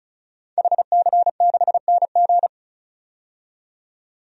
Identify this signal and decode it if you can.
Morse code, heard as a keyed tone, about 36 wpm, 700 Hz. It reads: HC6NG
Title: DXCC-100 Call Signs - 5 Characters - 35wpm